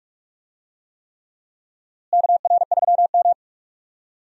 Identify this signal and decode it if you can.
Morse code, heard as a keyed tone, about 45 wpm, 700 Hz. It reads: KR3M
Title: USA Call Signs - 4 Characters - Repeated 3x - 45wpm